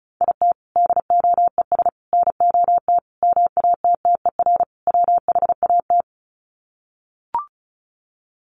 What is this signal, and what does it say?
Morse code, heard as a keyed tone, about 34 wpm, 700 Hz. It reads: IT DOES NOT MATTER WHAT
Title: Sets of 5 Words Encore 35wpm